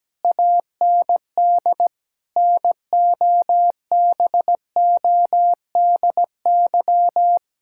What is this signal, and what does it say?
Morse code, heard as a keyed tone, about 17 wpm, 700 Hz. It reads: AND NOBODY
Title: Sentences from Top 1600 Words 17wpm